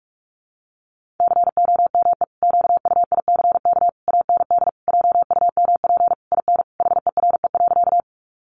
Morse code, heard as a keyed tone, about 44 wpm, 700 Hz. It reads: COME QUICK AND JUMP IN HERE.
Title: Sentences from Top 1300 Words 45wpm